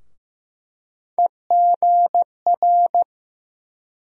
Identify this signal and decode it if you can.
Morse code, heard as a keyed tone, about 15 wpm, 700 Hz. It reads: EGR
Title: USA Call Sign Postfixes - 3 Letters 15wpm